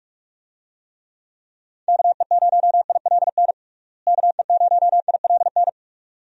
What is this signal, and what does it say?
Morse code, heard as a keyed tone, about 41 wpm, 700 Hz. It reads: KE0ILN KE0ILN
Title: USA Call Signs - 6 Characters - Repeated 2x - 45wpm